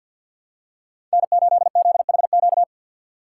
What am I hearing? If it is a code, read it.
Morse code, heard as a keyed tone, about 52 wpm, 700 Hz. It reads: N8ZHQ
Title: USA Call Signs - 2x Character Spacing - 50wpm